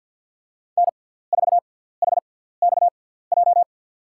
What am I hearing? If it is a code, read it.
Morse code, heard as a keyed tone, about 54 wpm, spaced as for 17 wpm, 700 Hz. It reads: N4HXJ
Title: USA Call Signs - 5 Characters - 6x Character Spacing - 50wpm